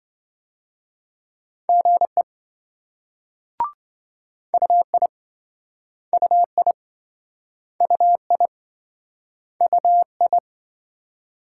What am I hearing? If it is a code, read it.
Morse code, heard as a keyed tone, about 27 wpm, 700 Hz. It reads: GE UI UI UI UI